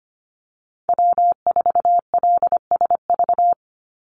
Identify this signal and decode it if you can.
Morse code, heard as a keyed tone, about 25 wpm, 700 Hz. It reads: W4LSV
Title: Call Signs - 25wpm